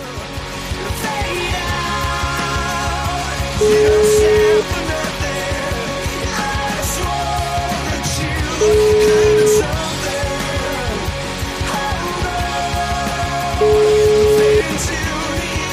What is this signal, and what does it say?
Ringback tone (Europe, 425 Hz), with noise.